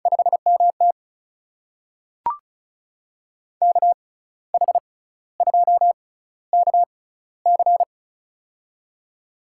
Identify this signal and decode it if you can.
Morse code, heard as a keyed tone, about 36 wpm, spaced as for 15 wpm, 700 Hz. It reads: 5MT KH2KC
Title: USA Call Signs - 5 Characters - 6x Character Spacing - 35wpm